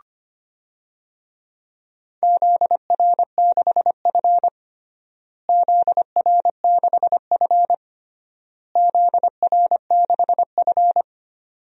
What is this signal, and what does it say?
Morse code, heard as a keyed tone, about 25 wpm, 700 Hz. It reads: ZR6F ZR6F ZR6F
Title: DXCC-100 Call Signs - 4 Characters - Repeated 3x - 25wpm